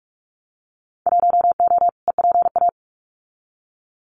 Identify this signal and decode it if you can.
Morse code, heard as a keyed tone, about 46 wpm, 700 Hz. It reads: 1O EPA
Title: Winter Field Day Exchanges 45wpm